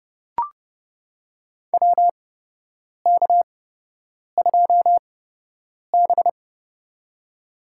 Morse code, heard as a keyed tone, about 31 wpm, spaced as for 8 wpm, 700 Hz. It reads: WK2B
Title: USA Call Signs - 8x Character Spacing - 30wpm